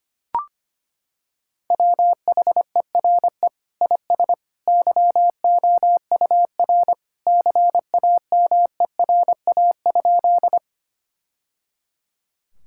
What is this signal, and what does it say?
Morse code, heard as a keyed tone, about 25 wpm, 700 Hz. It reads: WHERE IS YOUR CAMERA?
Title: Sentences from Top 1000 Words - Review Part 5 - 25wpm